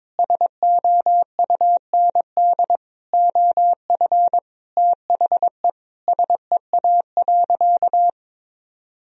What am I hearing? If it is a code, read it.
Morse code, heard as a keyed tone, about 22 wpm, 700 Hz. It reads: SOUND OF THE SEA.
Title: Sentences from Top 1000 Words - Review Part 1 - 22wpm